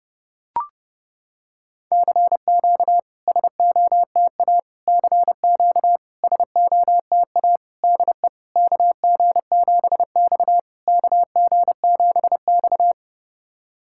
Morse code, heard as a keyed tone, about 30 wpm, 690 Hz. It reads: CQ SOTA CQ SOTA DE KG7X KG7X
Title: SOTA QSO Practice 30wpm